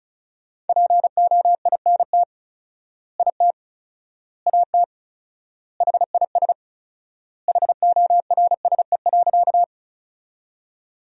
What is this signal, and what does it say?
Morse code, heard as a keyed tone, about 36 wpm, 700 Hz. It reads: POINT IT AT HIS HORSE.
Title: 4x Word-Spacing - Sentences from Top 300 Words - 35wpm